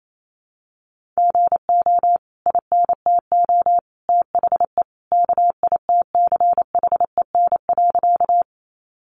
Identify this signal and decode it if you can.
Morse code, heard as a keyed tone, about 28 wpm, 700 Hz. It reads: GO INTO THE KITCHEN.
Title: Sentences from Top 1400 Words 28wpm